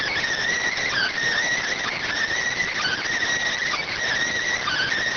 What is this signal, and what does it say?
Emergency Alert System (SAME) tones (header bursts), heard as harsh buzzing bursts. It is heard from 0.6 to 4.9 s.